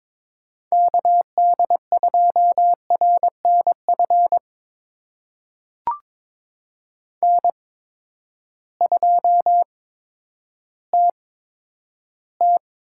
Morse code, heard as a keyed tone, about 22 wpm, spaced as for 11 wpm, 700 Hz. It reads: KD2RNF N2TT
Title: USA Call Signs - 8x Character Spacing - 22wpm